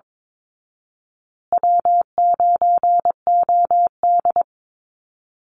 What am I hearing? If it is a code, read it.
Morse code, heard as a keyed tone, about 22 wpm, 690 Hz. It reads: W9OD
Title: Call Signs Repeated 3x - 22wpm